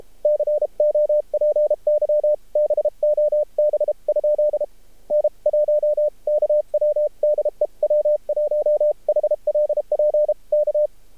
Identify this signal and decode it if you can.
Morse code, heard as a keyed tone, about 52 wpm, spaced as for 37 wpm, 600 Hz. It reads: COPY BOB? N1KW DE W1HLP K
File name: archived / CRSnet